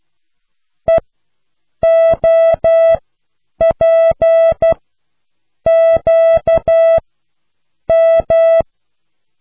Morse code, heard as a keyed tone, about 12 wpm, spaced as for 7 wpm, 660 Hz. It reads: EOPQM